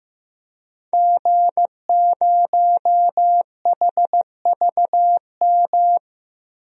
Morse code, heard as a keyed tone, about 15 wpm, 700 Hz. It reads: G0HVM
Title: DXCC-100 Call Signs - 5 Characters - Repeated 3x - 15wpm